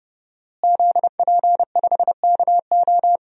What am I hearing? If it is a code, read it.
Morse code, heard as a keyed tone, about 30 wpm, 700 Hz. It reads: ZP5KO
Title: DXCC-100 Call Signs - 5 Characters - Repeated 3x - 30wpm